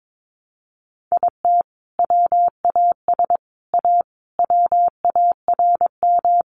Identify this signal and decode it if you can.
Morse code, heard as a keyed tone, about 22 wpm, 700 Hz. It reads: IT WAS A WARM
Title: Sentences from Top 500 Words 22wpm